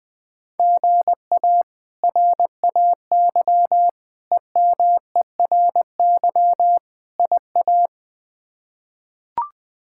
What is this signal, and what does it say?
Morse code, heard as a keyed tone, about 20 wpm, 700 Hz. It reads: GA RAY EMERY IA
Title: K1USN SST QSO Practice - 20wpm